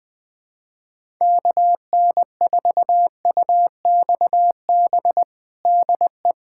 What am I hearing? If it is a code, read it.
Morse code, heard as a keyed tone, about 21 wpm, 700 Hz. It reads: KN4UXB DE
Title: USA Callsign DE Callsign - 6 Characters - 20wpm